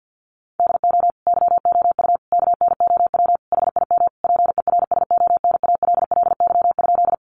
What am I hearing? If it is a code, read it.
Morse code, heard as a keyed tone, about 50 wpm, 700 Hz. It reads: DO YOU KNOW HIM PERSONALLY?
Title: Sentences from Top 1400 Words 50wpm